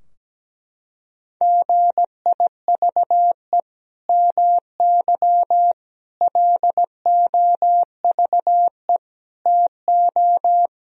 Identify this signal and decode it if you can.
Morse code, heard as a keyed tone, about 17 wpm, 700 Hz. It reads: GIVE MY LOVE TO
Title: Sentences from Top 1000 Words - Review Part 4 - 17wpm